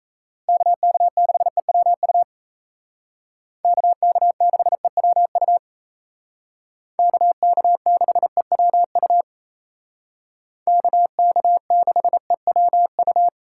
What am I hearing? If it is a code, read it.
Morse code, heard as a keyed tone, about 36 wpm, 700 Hz. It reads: KK6EWU KK6EWU KK6EWU KK6EWU